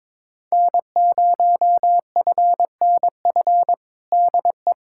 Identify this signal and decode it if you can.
Morse code, heard as a keyed tone, about 22 wpm, 700 Hz. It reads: N0FNF DE
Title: USA Callsign DE Callsign - 5 Characters - 22wpm